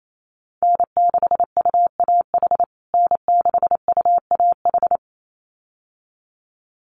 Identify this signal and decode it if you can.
Morse code, heard as a keyed tone, about 28 wpm, 700 Hz. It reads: N6UAH N6UAH
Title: USA Call Signs Repeated 2x 28wpm